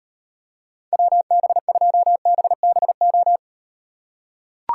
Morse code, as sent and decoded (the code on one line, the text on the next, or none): .-- -... ..--- -... -... ---
WB2BBO